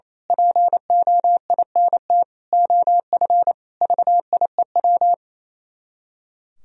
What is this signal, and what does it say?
Morse code, heard as a keyed tone, about 26 wpm, 700 Hz. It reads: POINT OF VIEW
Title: Speed-Racing - Sets of 5 Words 28wpm